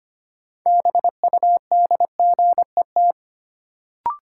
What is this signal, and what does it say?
Morse code, heard as a keyed tone, about 25 wpm, 700 Hz. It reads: BUDGET
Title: Sentences from Top 1200 Words - New Word Drill - 25wpm